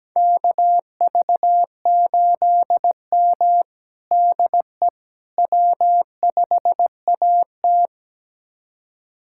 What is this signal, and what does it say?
Morse code, heard as a keyed tone, about 17 wpm, 700 Hz. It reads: KV8M DE W5AT
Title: USA Callsign DE Callsign - 4 Characters - 17wpm